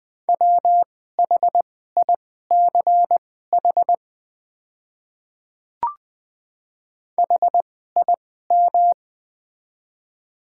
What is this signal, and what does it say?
Morse code, heard as a keyed tone, about 21 wpm, spaced as for 15 wpm, 690 Hz. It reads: WHICH HIM
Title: 2x Character Spacing - Top 100 Words - 20wpm